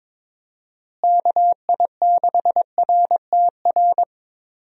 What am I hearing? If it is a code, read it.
Morse code, heard as a keyed tone, about 21 wpm, 700 Hz. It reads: KI6RTR
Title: USA Call Signs - 6 Characters - Repeated 2x - 22wpm